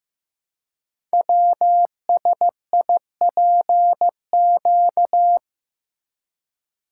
Morse code, heard as a keyed tone, about 15 wpm, 700 Hz. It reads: WSIPQ
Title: Code Groups - 5 Letters - 15wpm